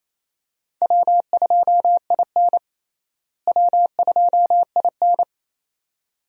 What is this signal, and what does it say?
Morse code, heard as a keyed tone, about 28 wpm, 700 Hz. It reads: W2IN W2IN
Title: USA Call Signs - 4 Characters - Repeated 2x - 28wpm